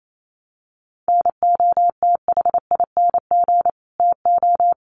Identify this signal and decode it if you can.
Morse code, heard as a keyed tone, about 28 wpm, 700 Hz. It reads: NOTHING TO
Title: Sentences from Top 1300 Words 28wpm